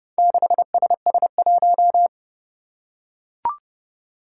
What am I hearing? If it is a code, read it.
Morse code, heard as a keyed tone, about 29 wpm, 700 Hz. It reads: 6SS1
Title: Four Challenging Letter-Number Combinations 30wpm